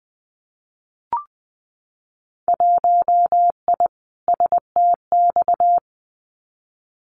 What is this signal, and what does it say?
Morse code, heard as a keyed tone, about 20 wpm, 700 Hz. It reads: 1I STX
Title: Winter Field Day Exchanges 20wpm